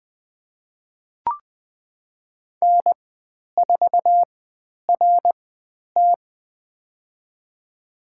Morse code, heard as a keyed tone, about 20 wpm, spaced as for 10 wpm, 700 Hz. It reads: N4RT
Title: 095 Lesson 16g - Letter c - Call Signs 20wpm